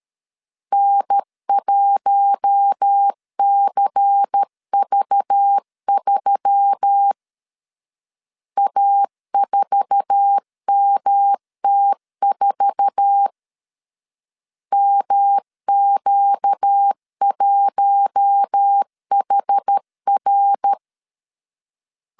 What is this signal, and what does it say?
Morse code, heard as a keyed tone, about 13 wpm, 800 Hz. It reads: N1CV3 A4MT4 MQ1HR
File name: LEZIONI / 8 YJ45 / 8 YJ4 12wpm 2151